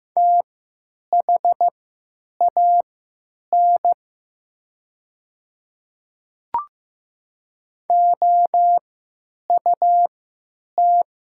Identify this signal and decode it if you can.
Morse code, heard as a keyed tone, about 15 wpm, spaced as for 9 wpm, 690 Hz. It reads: THAN OUT
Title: 3x Character Spacing - Top 100 Words - 15wpm